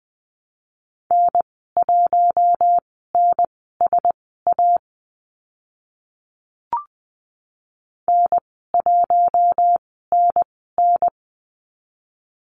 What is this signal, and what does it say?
Morse code, heard as a keyed tone, about 20 wpm, spaced as for 15 wpm, 700 Hz. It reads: N1NSA N1NN
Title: 042 Lesson 8g - Number 1 - Call Signs 20wpm